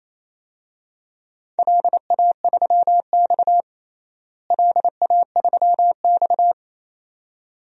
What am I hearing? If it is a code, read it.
Morse code, heard as a keyed tone, about 28 wpm, 700 Hz. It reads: LA3X LA3X
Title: DXCC-100 Call Signs - 4 Characters - Repeated 2x - 28wpm